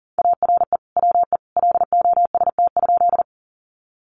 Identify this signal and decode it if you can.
Morse code, heard as a keyed tone, about 40 wpm, 700 Hz. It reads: ARE WE LOST?
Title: Sentences from Top 1400 Words 40wpm